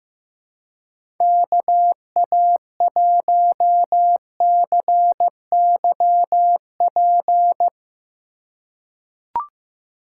Morse code, heard as a keyed tone, about 15 wpm, 700 Hz. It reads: KA1CYP
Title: USA Call Signs - 3x Character Spacing - 15wpm